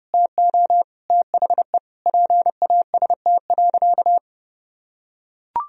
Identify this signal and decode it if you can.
Morse code, heard as a keyed tone, about 30 wpm, 700 Hz. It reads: TO THE PAST.
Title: Sentences from Top 1000 Words - Review Part 3 - 30wpm